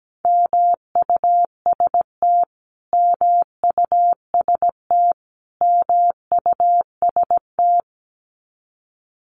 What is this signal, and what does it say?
Morse code, heard as a keyed tone, about 17 wpm, 700 Hz. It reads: MUST MUST MUST
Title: Top 200 Words Repeated 17wpm